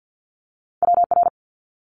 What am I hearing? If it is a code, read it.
Morse code, heard as a keyed tone, about 50 wpm, 700 Hz. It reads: WR
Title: Multi-letter Phonemes 50wpm